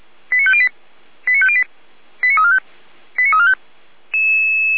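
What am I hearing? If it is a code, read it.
Five-tone selective call (ZVEI), heard as quick runs of five tones, tones of about 70 ms. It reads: 89508 89508 89345 89345
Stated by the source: ZVEI five-tone selective call 89371, received by SDR on the 455 channel.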